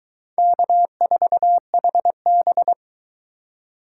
Morse code, heard as a keyed tone, about 23 wpm, 700 Hz. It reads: K4HB